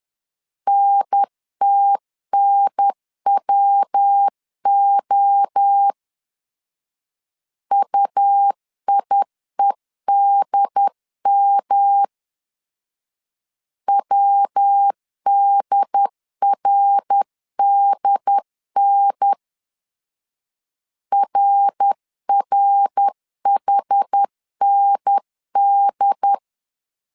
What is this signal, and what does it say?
Morse code, heard as a keyed tone, about 10 wpm, 800 Hz. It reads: NTNWO UIEDM WDRDN RRHND
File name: 10wpm